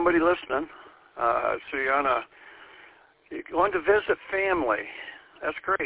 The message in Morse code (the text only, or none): none